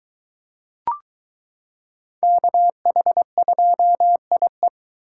Morse code, heard as a keyed tone, about 22 wpm, 700 Hz. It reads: KH2IE